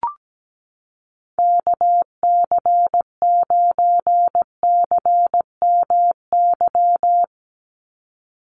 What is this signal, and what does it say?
Morse code, heard as a keyed tone, about 17 wpm, 700 Hz. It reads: KC9CMY
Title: USA Call Signs 17wpm